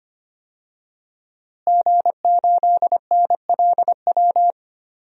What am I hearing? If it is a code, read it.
Morse code, heard as a keyed tone, about 25 wpm, 700 Hz. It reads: G8NLW
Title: DXCC-100 Call Signs - 5 Characters - Repeated 2x - 25wpm